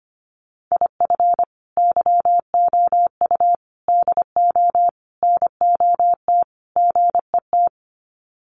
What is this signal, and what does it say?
Morse code, heard as a keyed tone, about 25 wpm, 700 Hz. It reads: IF YOU DO NOT GET
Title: Sets of 5 Words Encore2 25wpm